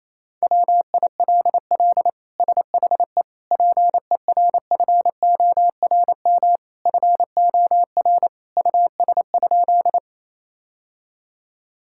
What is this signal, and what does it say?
Morse code, heard as a keyed tone, about 28 wpm, 700 Hz. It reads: WILL SHE PERFORM FOR US?
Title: Sentences from Top 1000 Words - Review Part 5 - 28wpm